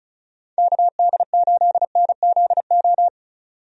Morse code, heard as a keyed tone, about 36 wpm, 700 Hz. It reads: KD8NZO
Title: USA Call Signs - 6 Characters - 35wpm